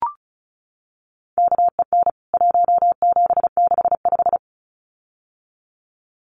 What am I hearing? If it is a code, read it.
Morse code, heard as a keyed tone, about 34 wpm, 700 Hz. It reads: KEN 1765
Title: CWT Exchange 35wpm